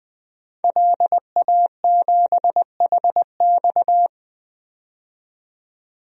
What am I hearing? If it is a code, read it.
Morse code, heard as a keyed tone, about 20 wpm, 700 Hz. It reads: LA7HX
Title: DXCC-100 Call Signs - 5 Characters - 20wpm